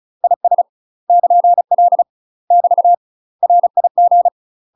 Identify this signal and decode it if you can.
Morse code, heard as a keyed tone, about 35 wpm, 700 Hz. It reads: IS <KN>L <BT> RIG
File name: UK MASTER QSOs TRAINING / 35WPM / UK MASTER QSOs 0022